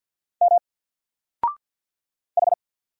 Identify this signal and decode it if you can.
Morse code, heard as a keyed tone, about 50 wpm, 700 Hz. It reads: M H
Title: Single Letter-Number 50wpm